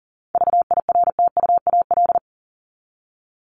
Morse code, heard as a keyed tone, about 40 wpm, 700 Hz. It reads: VIRTUAL
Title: LY Suffix - Top 100 Words with Base Words 40wpm